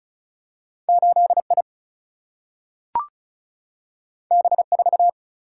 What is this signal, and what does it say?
Morse code, heard as a keyed tone, about 36 wpm, 700 Hz. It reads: 8I B4
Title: Single-Letter Single-Number 35wpm